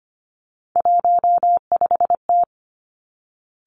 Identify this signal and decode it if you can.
Morse code, heard as a keyed tone, about 25 wpm, 700 Hz. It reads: 15T
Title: Three Challenging Letter-Number Combinations 25wpm